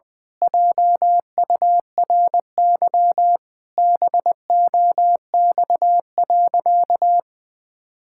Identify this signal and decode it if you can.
Morse code, heard as a keyed tone, about 20 wpm, 700 Hz. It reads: JURY BOX.
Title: Pangram Challenge v2 - 20wpm